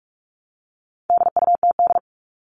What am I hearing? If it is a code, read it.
Morse code, heard as a keyed tone, about 44 wpm, 700 Hz. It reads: DUTD
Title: All Cut Numbers - 4-Digit 45wpm